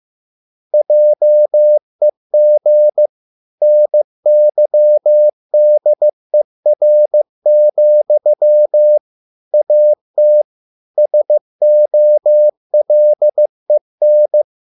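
Morse code, heard as a keyed tone, about 15 wpm, 600 Hz. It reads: JEG NYDER, AT SOLEN
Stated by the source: Morse code at a slow 15 wpm from CW med Gnister.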